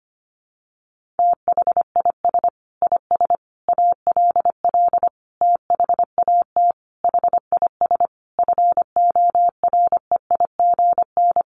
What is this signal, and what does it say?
Morse code, heard as a keyed tone, about 25 wpm, 700 Hz. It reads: THIS IS ALL THAT HIS FOREIGN
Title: Sentences from Top 1400 Words 25wpm